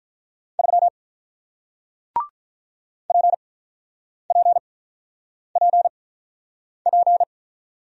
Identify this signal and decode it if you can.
Morse code, heard as a keyed tone, about 48 wpm, spaced as for 9 wpm, 700 Hz. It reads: 3 PPPP